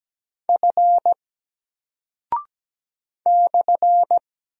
Morse code, heard as a keyed tone, about 17 wpm, 700 Hz. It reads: F /
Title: Single Character (Letter:Number:Punctuation:Prosign) 17wpm